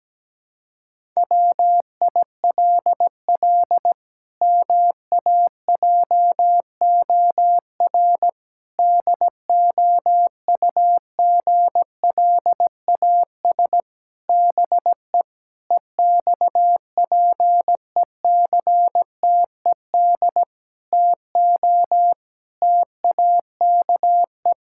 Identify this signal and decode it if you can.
Morse code, heard as a keyed tone, about 17 wpm, 700 Hz. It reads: WILL MAJOR DOUGLAS BE EXPECTED TO TAKE
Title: Pangram Challenge v2 - 17wpm